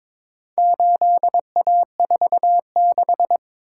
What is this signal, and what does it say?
Morse code, heard as a keyed tone, about 22 wpm, 700 Hz. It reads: 8A46
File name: practice / 22wpm / Four Letter-Number Combinations - 22wpm